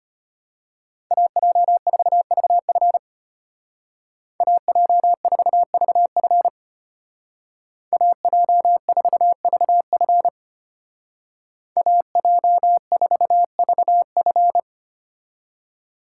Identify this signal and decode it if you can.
Morse code, heard as a keyed tone, about 30 wpm, 700 Hz. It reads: AJ4VF AJ4VF AJ4VF AJ4VF
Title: Speed Racing - USA 2x2 Call Signs 25wpm